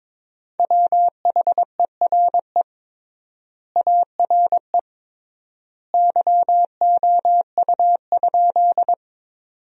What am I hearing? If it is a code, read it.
Morse code, heard as a keyed tone, about 22 wpm, 700 Hz. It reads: WHERE ARE YOU?
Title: 3x Word-Spacing - Sentences from Top 200 Words - 22wpm